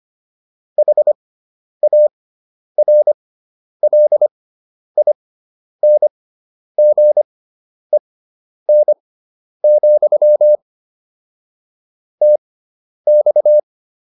Morse code, heard as a keyed tone, about 25 wpm, spaced as for 10 wpm, 600 Hz. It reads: HARLINGEN, TX
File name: QSO304 10